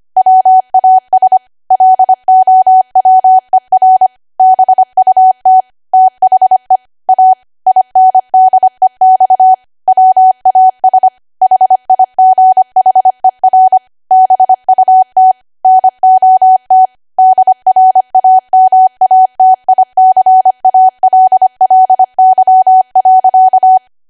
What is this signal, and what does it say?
Morse code, heard as a keyed tone, about 25 wpm, 750 Hz. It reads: WAS LOWER BUT THE A INDEX WAS HIGHER BUT NOT DRAMATICALLY.